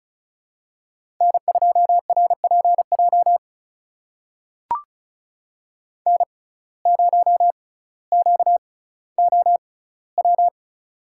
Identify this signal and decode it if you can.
Morse code, heard as a keyed tone, about 36 wpm, spaced as for 18 wpm, 700 Hz. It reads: N2RPJ N0QOW
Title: USA Call Signs - 5 Characters - 6x Character Spacing - 35wpm